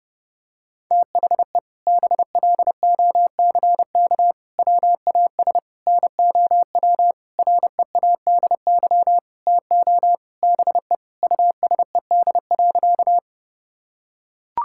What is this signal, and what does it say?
Morse code, heard as a keyed tone, about 32 wpm, 700 Hz. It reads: THE BLOCK WAS NOW READY TO BE USED.
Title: Sentences from Top 1600 Words 30wpm